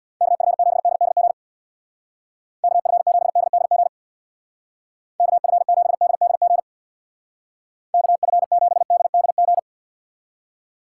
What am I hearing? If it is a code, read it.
Morse code, heard as a keyed tone, about 60 wpm, 700 Hz. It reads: KF7DDG KF7DDG KF7DDG KF7DDG
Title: Speed Racing - USA 2x3 Call Signs 50wpm